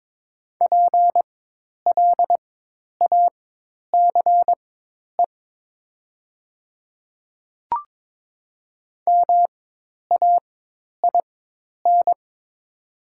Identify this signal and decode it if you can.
Morse code, heard as a keyed tone, about 22 wpm, spaced as for 10 wpm, 700 Hz. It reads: PLACE MAIN
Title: 4x Character Spacing - Top 300 Words - 22wpm